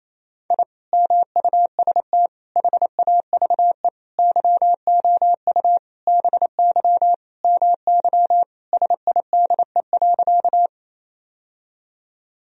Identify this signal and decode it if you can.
Morse code, heard as a keyed tone, about 28 wpm, 700 Hz. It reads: I MUST HAVE YOU BY MY SIDE.
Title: Sentences from Top 1000 Words - Review Part 1 - 28wpm